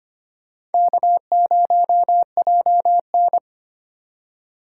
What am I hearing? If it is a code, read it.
Morse code, heard as a keyed tone, about 33 wpm, 700 Hz. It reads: K0JN